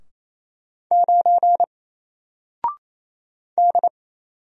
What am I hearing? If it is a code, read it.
Morse code, heard as a keyed tone, about 29 wpm, 700 Hz. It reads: 9 D
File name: Single-Letter-Number-28wpm